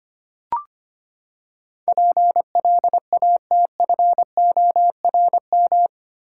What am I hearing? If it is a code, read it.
Morse code, heard as a keyed tone, about 25 wpm, 700 Hz. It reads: PLATFORM
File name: Eight Letter Words - 25wpm